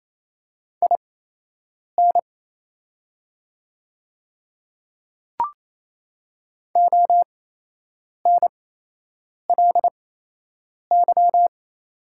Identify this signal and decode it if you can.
Morse code, heard as a keyed tone, about 27 wpm, spaced as for 8 wpm, 700 Hz. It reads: IN ONLY
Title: 8x Character Spacing - Top 200 Words - 28wpm